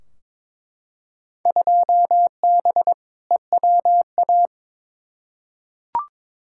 Morse code, heard as a keyed tone, about 22 wpm, 700 Hz. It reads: 2B EWA
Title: Field Day Exchanges Encore - 22wpm